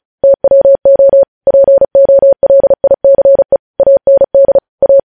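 Morse code, heard as a keyed tone, about 36 wpm, 550 Hz. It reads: TWO POLICE AND A
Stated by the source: Morse news headlines practice at 35.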